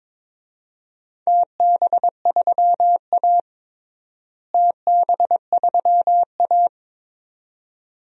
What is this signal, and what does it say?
Morse code, heard as a keyed tone, about 22 wpm, 700 Hz. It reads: TB3A TB3A
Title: DXCC-100 Call Signs - 4 Characters - Repeated 2x - 22wpm